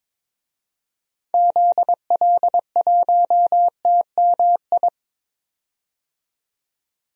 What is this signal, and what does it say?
Morse code, heard as a keyed tone, about 22 wpm, 700 Hz. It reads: ZL1TMI
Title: DXCC-100 Call Signs - 6 Characters - 22wpm